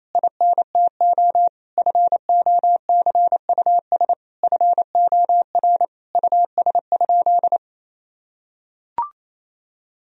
Morse code, heard as a keyed tone, about 28 wpm, 700 Hz. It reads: INTO FOCUS FOR US?
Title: Sentences from Top 1000 Words - Review Part 4 - 28wpm